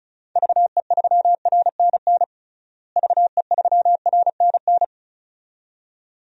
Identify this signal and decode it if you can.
Morse code, heard as a keyed tone, about 34 wpm, 700 Hz. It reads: VE3RNN VE3RNN
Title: DXCC-100 Call Signs - 6 Characters - Repeated 2x - 35wpm